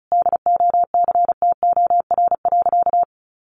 Morse code, heard as a keyed tone, about 34 wpm, 700 Hz. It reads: DOCTOR.
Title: Sentences from Top 1500 Words 35wpm